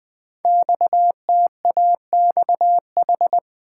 Morse code, heard as a keyed tone, about 20 wpm, 700 Hz. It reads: XTAXH
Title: Code Groups - 5 Letters - 20wpm